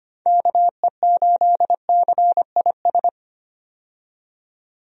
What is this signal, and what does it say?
Morse code, heard as a keyed tone, about 25 wpm, 700 Hz. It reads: KE8CIS
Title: USA Call Signs - 6 Characters - 25wpm